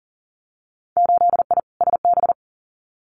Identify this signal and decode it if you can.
Morse code, heard as a keyed tone, about 38 wpm, 700 Hz. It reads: 8I SB
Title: Winter Field Day Exchanges Encore 40wpm